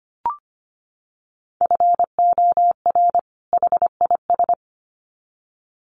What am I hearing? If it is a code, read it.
Morse code, heard as a keyed tone, about 25 wpm, 700 Hz. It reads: FOR HIS
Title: Sets of 2 Words 25wpm